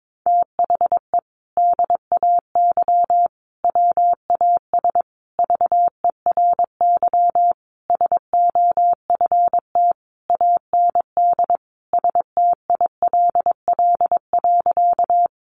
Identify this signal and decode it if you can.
Morse code, heard as a keyed tone, about 22 wpm, 700 Hz. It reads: THE DAY WAS VERY SOFT AND STILL.
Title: Sentences from Top 1500 Words 22wpm